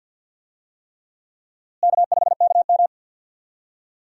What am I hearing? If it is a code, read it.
Morse code, heard as a keyed tone, about 52 wpm, 700 Hz. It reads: K5KM
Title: USA Call Signs - 4 Characters - Repeated 3x - 50wpm